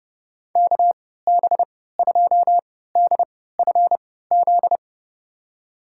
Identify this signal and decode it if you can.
Morse code, heard as a keyed tone, about 30 wpm, spaced as for 17 wpm, 700 Hz. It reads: KB2DFZ
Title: USA Call Signs - 3x Character Spacing - 30wpm